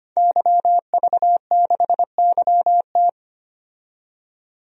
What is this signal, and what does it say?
Morse code, heard as a keyed tone, about 25 wpm, 700 Hz. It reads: YV6YT
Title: DXCC-100 Call Signs - 5 Characters - 25wpm